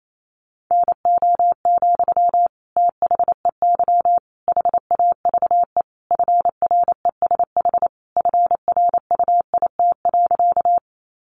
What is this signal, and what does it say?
Morse code, heard as a keyed tone, about 28 wpm, 700 Hz. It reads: NO, THEY HAVE FRESH FRUIT.
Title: Sentences from Top 1300 Words 28wpm